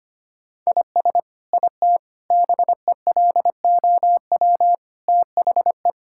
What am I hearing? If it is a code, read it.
Morse code, heard as a keyed tone, about 25 wpm, 700 Hz. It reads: IS IT BELOW THE
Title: Sentences from Top 1000 Words - Review Part 5 - 25wpm